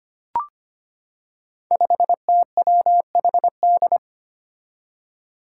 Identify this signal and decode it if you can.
Morse code, heard as a keyed tone, about 25 wpm, 700 Hz. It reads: 5TWHD
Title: Five Challenging Letter-Number Combinations 25wpm